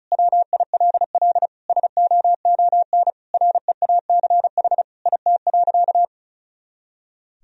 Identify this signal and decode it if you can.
Morse code, heard as a keyed tone, about 36 wpm, 700 Hz. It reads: WILL SOON REACH IT.
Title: Sentences from Top 400 Words and Abbreviations 35wpm